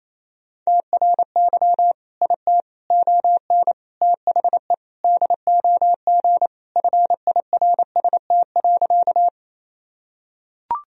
Morse code, heard as a keyed tone, about 28 wpm, 700 Hz. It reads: TRY IT ON THE DOG FIRST.
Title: Sentences from Top 1000 Words - Review Part 2 - 28wpm